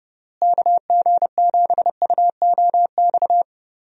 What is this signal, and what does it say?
Morse code, heard as a keyed tone, about 30 wpm, 700 Hz. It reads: KG7UOX
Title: USA Call Signs - 6 Characters - Repeated 2x - 30wpm